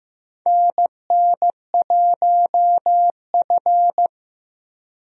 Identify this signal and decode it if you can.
Morse code, heard as a keyed tone, about 15 wpm, 700 Hz. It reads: NN1F
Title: USA Call Signs - 4 Characters - Repeated 2x - 15wpm